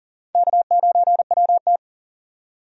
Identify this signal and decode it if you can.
Morse code, heard as a keyed tone, about 40 wpm, 700 Hz. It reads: K9WT
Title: USA Call Signs - 4 Characters - 40wpm